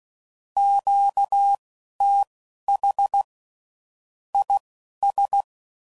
Morse code, heard as a keyed tone, about 16 wpm, spaced as for 11 wpm, 800 Hz. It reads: QTH IS
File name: CW Code Groups @10WPM / QSOs / QSO kc7haa